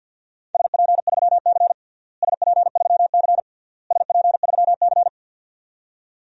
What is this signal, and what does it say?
Morse code, heard as a keyed tone, about 52 wpm, 700 Hz. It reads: SP3C SP3C SP3C
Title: DXCC-100 Call Signs - 4 Characters - Repeated 3x - 50wpm